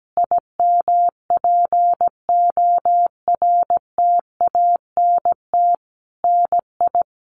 Morse code, heard as a keyed tone, about 17 wpm, 700 Hz. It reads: IMPORTANT NI
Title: Sentences from Top 1300 Words 17wpm